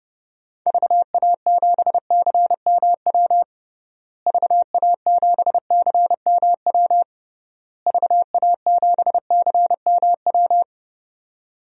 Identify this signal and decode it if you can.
Morse code, heard as a keyed tone, about 29 wpm, 700 Hz. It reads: VA7CMW VA7CMW VA7CMW
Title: DXCC-100 Call Signs - 6 Characters - Repeated 3x - 30wpm